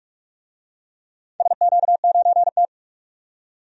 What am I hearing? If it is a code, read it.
Morse code, heard as a keyed tone, about 44 wpm, 700 Hz. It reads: SQ9T
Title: DXCC-100 Call Signs - 4 Characters - 45wpm